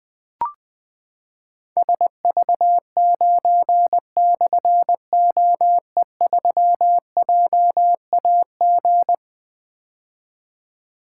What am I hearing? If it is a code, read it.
Morse code, heard as a keyed tone, about 21 wpm, 700 Hz. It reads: SV9/OE3JAG
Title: Difficult Call Signs 20wpm